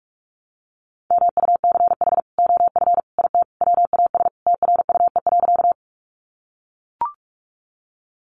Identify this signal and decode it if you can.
Morse code, heard as a keyed tone, about 44 wpm, 700 Hz. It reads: MUCH OF IT WAS TRUE.
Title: Sentences from Top 400 Words 45wpm